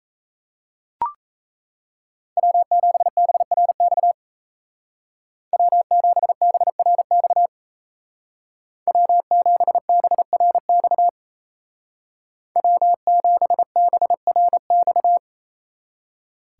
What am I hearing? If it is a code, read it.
Morse code, heard as a keyed tone, about 35 wpm, 700 Hz. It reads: W7BRX W7BRX W7BRX W7BRX
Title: Speed Racing - USA 1x3 Call Signs 28wpm